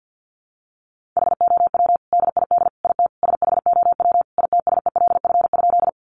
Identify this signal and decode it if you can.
Morse code, heard as a keyed tone, about 50 wpm, 700 Hz. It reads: HOW DID IT SHOW ITSELF?
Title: Sentences from Top 800 Words 50wpm